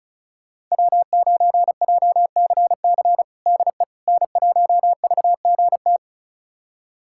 Morse code, heard as a keyed tone, about 36 wpm, 700 Hz. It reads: W9JCC DE N1VGT
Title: USA Callsign DE Callsign - 5 Characters - 35wpm